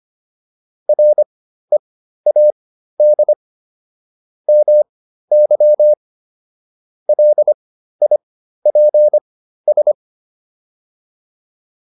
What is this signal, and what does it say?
Morse code, heard as a keyed tone, about 25 wpm, spaced as for 13 wpm, 600 Hz. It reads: READ MY LIPS